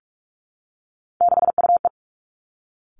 Morse code, heard as a keyed tone, about 46 wpm, 700 Hz. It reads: 6UE
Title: Three Challenging Letter-Number Combinations 45wpm